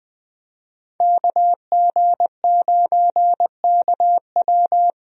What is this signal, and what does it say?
Morse code, heard as a keyed tone, about 21 wpm, 700 Hz. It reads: KG9KW
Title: USA Call Signs - 5 Characters - 20wpm